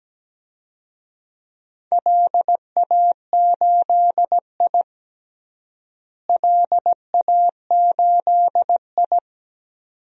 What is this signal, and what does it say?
Morse code, heard as a keyed tone, about 17 wpm, 700 Hz. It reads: LA8I LA8I
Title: DXCC-100 Call Signs - 4 Characters - Repeated 2x - 17wpm